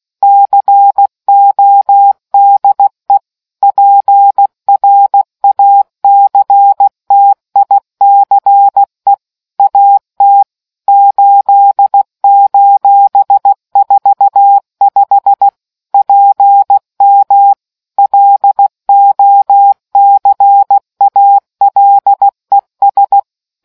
Morse code, heard as a keyed tone, about 16 wpm, 800 Hz. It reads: CODE PRACTICE AT 8:45 PM LOCALES